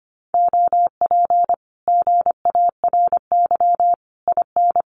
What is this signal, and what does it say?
Morse code, heard as a keyed tone, about 25 wpm, 700 Hz. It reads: OP GARY IN
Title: Common QSO Phrases Encore 25wpm